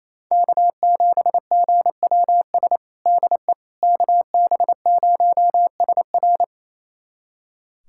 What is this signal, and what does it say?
Morse code, heard as a keyed tone, about 28 wpm, 700 Hz. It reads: K7GWS DE KB0SR
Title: USA Callsign DE Callsign - 5 Characters - 28wpm